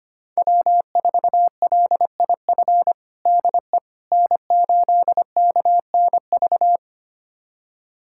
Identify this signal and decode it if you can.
Morse code, heard as a keyed tone, about 25 wpm, 700 Hz. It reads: W4LIF DE N8KNV
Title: USA Callsign DE Callsign - 5 Characters - 25wpm